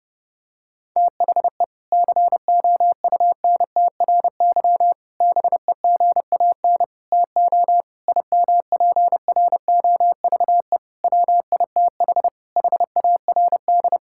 Morse code, heard as a keyed tone, about 32 wpm, 700 Hz. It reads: THE COUNTRY BEGAN TO IMPROVE WITH HARD
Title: Sentences from Top 1000 Words - Review Part 5 - 30wpm